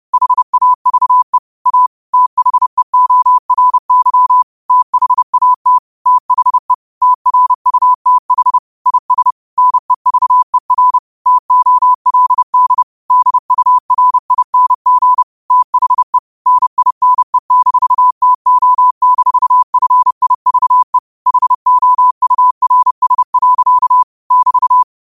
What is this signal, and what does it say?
Morse code, heard as a keyed tone, about 30 wpm, 1000 Hz. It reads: HAVE A THEORY THAT THE TRUTH IS NEVER TOLD DURING THE NINE-TO-FIVE HOURS. <BT>